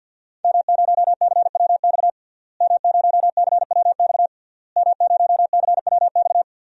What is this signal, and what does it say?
Morse code, heard as a keyed tone, about 49 wpm, 700 Hz. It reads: M0CWX M0CWX M0CWX
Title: DXCC-100 Call Signs - 5 Characters - Repeated 3x - 50wpm